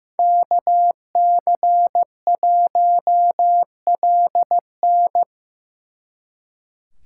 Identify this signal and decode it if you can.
Morse code, heard as a keyed tone, about 18 wpm, 700 Hz. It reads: KC1LN
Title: Speed Racing - USA 2x2 Call Signs 15wpm